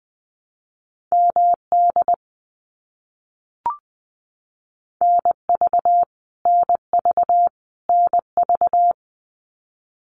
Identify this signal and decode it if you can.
Morse code, heard as a keyed tone, about 20 wpm, 700 Hz. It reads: MD NV NV NV
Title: US States Repeated 3x 20wpm